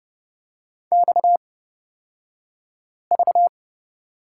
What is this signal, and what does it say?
Morse code, heard as a keyed tone, about 30 wpm, spaced as for 5 wpm, 700 Hz. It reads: XV
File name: Lesson 37d - Letter x - Two Characters 30wpm